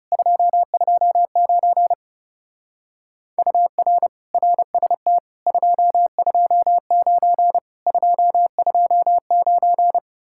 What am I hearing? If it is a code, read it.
Morse code, heard as a keyed tone, about 37 wpm, 700 Hz. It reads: 229 UR RST 229 229
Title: Speed-Racing Common QSO Phrases - 30wpm